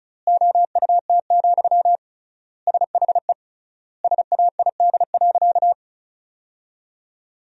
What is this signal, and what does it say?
Morse code, heard as a keyed tone, about 34 wpm, 700 Hz. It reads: OUT, SHE SAID.
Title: 3x Word-Spacing - Sentences from Top 500 Words - 35wpm